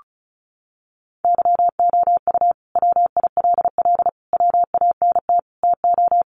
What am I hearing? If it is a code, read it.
Morse code, heard as a keyed tone, about 34 wpm, 700 Hz. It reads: YOU WILL WANT TO
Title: Sentences from Top 700 Words 35wpm